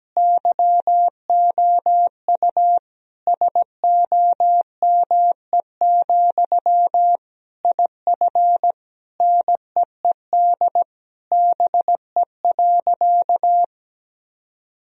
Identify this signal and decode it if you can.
Morse code, heard as a keyed tone, about 17 wpm, 700 Hz. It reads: YOU SOME, IF NEED BE.
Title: Sentences from Top 1000 Words - Review Part 1 - 17wpm